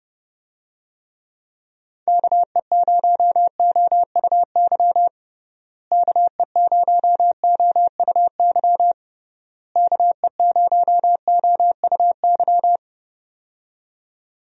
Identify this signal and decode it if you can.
Morse code, heard as a keyed tone, about 32 wpm, 700 Hz. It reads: KE0OUY KE0OUY KE0OUY
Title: USA Call Signs - 6 Characters - Repeated 3x - 30wpm